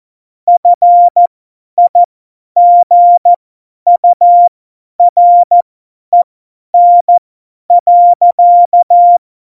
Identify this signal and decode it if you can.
Morse code, heard as a keyed tone, about 14 wpm, spaced as for 10 wpm, 700 Hz. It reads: FIGUREN.
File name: Morsekurs / Witze / Witz 04 August 2020, 50 F70